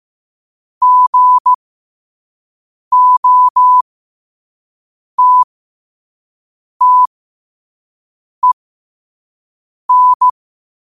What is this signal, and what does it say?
Morse code, heard as a keyed tone, about 15 wpm, spaced as for 5 wpm, 1000 Hz. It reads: GOTTEN
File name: qotdfiles / 5wpm